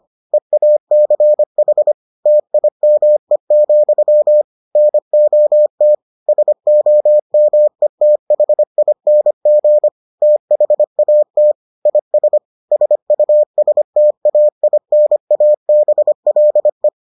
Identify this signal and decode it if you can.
Morse code, heard as a keyed tone, about 25 wpm, 600 Hz. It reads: EACH TIME, NOT SOMETHING THAT IS SUSTAINABLE